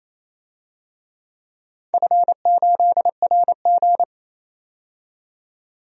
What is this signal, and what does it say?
Morse code, heard as a keyed tone, about 27 wpm, 690 Hz. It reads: F8RG
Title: DXCC-100 Call Signs - 4 Characters - 28wpm